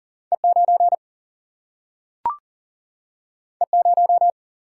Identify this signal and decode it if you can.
Morse code, heard as a keyed tone, about 40 wpm, 700 Hz. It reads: E 9 E 0
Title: Single-Letter Single-Number 40wpm